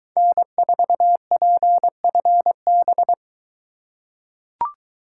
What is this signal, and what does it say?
Morse code, heard as a keyed tone, about 22 wpm, 700 Hz. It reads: N4PFB